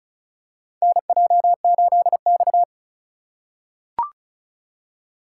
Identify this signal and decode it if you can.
Morse code, heard as a keyed tone, about 36 wpm, 700 Hz. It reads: NJ8X
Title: USA Call Signs - 4x Character Spacing - 35wpm